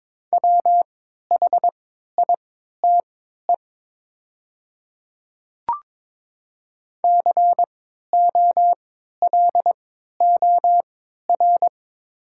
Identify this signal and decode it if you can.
Morse code, heard as a keyed tone, about 22 wpm, spaced as for 12 wpm, 700 Hz. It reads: WHITE COLOR
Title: 3x Character Spacing - Top 500 Words - 22wpm